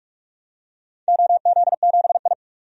Morse code, heard as a keyed tone, about 44 wpm, 700 Hz. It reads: OZ7I
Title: DXCC-100 Call Signs - 4 Characters - 45wpm